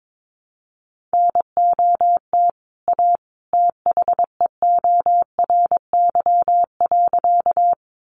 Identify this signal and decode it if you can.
Morse code, heard as a keyed tone, about 22 wpm, 700 Hz. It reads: NOT A THEORY.
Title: Sentences from Top 600 Words 22wpm